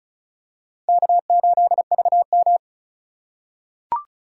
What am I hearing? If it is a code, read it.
Morse code, heard as a keyed tone, about 34 wpm, 700 Hz. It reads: K8VM
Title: USA Call Signs - 4 Characters - 4x Character Spacing - 35wpm